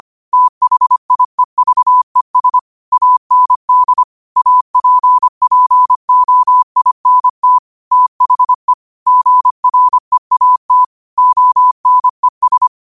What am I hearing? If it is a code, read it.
Morse code, heard as a keyed tone, about 25 wpm, 1000 Hz. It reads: THIEVES AND APPOINT THE GREAT ONES
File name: qotdfiles / 25wpm